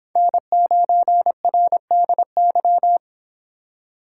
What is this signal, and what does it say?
Morse code, heard as a keyed tone, about 30 wpm, 700 Hz. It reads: N9RDY